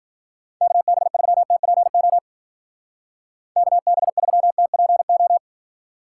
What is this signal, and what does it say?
Morse code, heard as a keyed tone, about 46 wpm, 700 Hz. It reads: KB3TPO KB3TPO
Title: Speed Racing - USA 2x3 Call Signs 40wpm